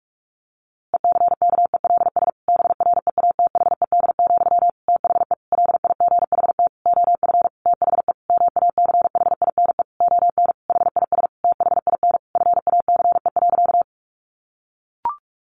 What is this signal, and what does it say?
Morse code, heard as a keyed tone, about 44 wpm, 700 Hz. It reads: ECKELS BREATHED, THE LIGHT OF THE MACHINE ON HIS THIN FACE.
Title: A Sound Of Thunder 45wpm